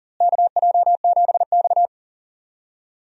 Morse code, heard as a keyed tone, about 40 wpm, 700 Hz. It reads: KJ7X
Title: USA Call Signs - 4 Characters - Repeated 3x - 40wpm